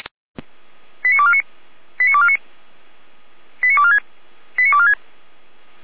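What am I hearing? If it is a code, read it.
Five-tone selective call (ZVEI), heard as quick runs of five tones, tones of about 70 ms. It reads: 89249 89249 89246 89246
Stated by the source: ZVEI five-tone call 89187 received by SDR.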